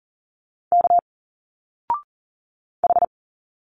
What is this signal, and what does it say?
Morse code, heard as a keyed tone, about 40 wpm, 700 Hz. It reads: K H
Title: Single Letters 40wpm